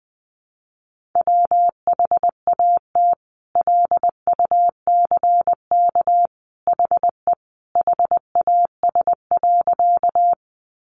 Morse code, heard as a keyed tone, about 20 wpm, 700 Hz. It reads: WHAT LUCK HE HAS.
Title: Sentences from Top 1500 Words 20wpm